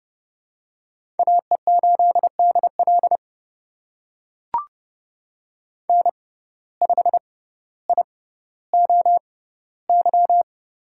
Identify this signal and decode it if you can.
Morse code, heard as a keyed tone, about 30 wpm, spaced as for 15 wpm, 700 Hz. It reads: AE8DL N5IOY
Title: USA Call Signs - 5 Characters - 6x Character Spacing - 30wpm